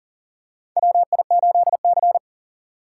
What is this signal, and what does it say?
Morse code, heard as a keyed tone, about 40 wpm, 700 Hz. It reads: WI8C
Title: USA Call Signs - 4 Characters - 4x Character Spacing - 40wpm